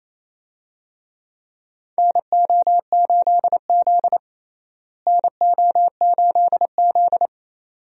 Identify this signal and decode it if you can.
Morse code, heard as a keyed tone, about 28 wpm, 700 Hz. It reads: NO8Z NO8Z
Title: USA Call Signs - 4 Characters - Repeated 2x - 28wpm